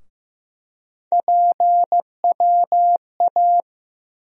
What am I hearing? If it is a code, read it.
Morse code, heard as a keyed tone, about 15 wpm, 700 Hz. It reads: PWA
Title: USA Call Sign Postfixes - 3 Letters 15wpm